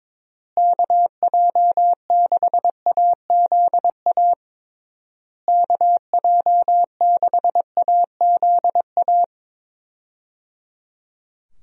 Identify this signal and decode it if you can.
Morse code, heard as a keyed tone, about 22 wpm, 700 Hz. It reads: KJ6AZA KJ6AZA
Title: USA Call Signs - 6 Characters - Repeated 3x - 22wpm